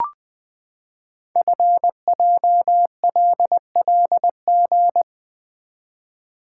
Morse code, heard as a keyed tone, about 20 wpm, 700 Hz. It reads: FJLLG
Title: Code Groups - 5 Letters - 20wpm